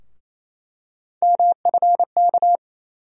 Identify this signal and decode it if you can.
Morse code, heard as a keyed tone, about 28 wpm, 700 Hz. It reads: MFK
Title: USA Call Sign Postfixes - 3 Letters 28wpm